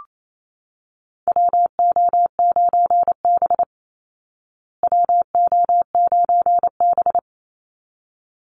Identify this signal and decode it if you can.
Morse code, heard as a keyed tone, about 28 wpm, 700 Hz. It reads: WO9B WO9B
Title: Call Signs Repeated 3x - 28wpm